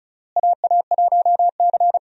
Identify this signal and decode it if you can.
Morse code, heard as a keyed tone, about 36 wpm, 700 Hz. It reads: AA1C
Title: USA Call Signs - 4 Characters - Repeated 2x - 35wpm